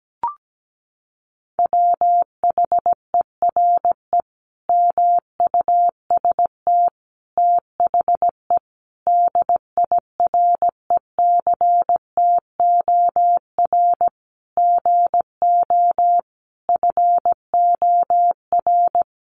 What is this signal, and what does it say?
Morse code, heard as a keyed tone, about 17 wpm, 700 Hz. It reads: WHERE MUST THE DIRECTOR GO FOR
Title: Sentences from Top 1300 Words 17wpm